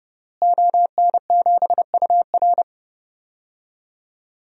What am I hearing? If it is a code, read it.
Morse code, heard as a keyed tone, about 30 wpm, 700 Hz. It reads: ON7UR
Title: DXCC-100 Call Signs - 5 Characters - Repeated 3x - 30wpm